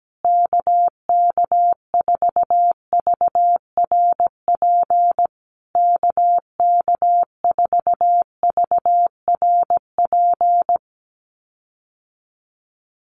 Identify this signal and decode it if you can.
Morse code, heard as a keyed tone, about 17 wpm, 700 Hz. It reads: KK4VRP KK4VRP
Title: USA Call Signs Repeated 3x 17wpm